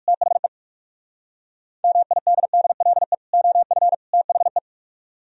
Morse code, heard as a keyed tone, about 65 wpm, spaced as for 55 wpm, 700 Hz. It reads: THE MIDDLE OF THE
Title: Speed-Racing - Sets of 3 Words 45wpm